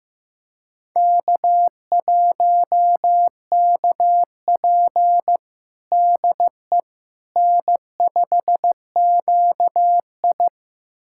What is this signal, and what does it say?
Morse code, heard as a keyed tone, about 15 wpm, 700 Hz. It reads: K1KP DE N5QI
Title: USA Callsign DE Callsign - 4 Characters - 15wpm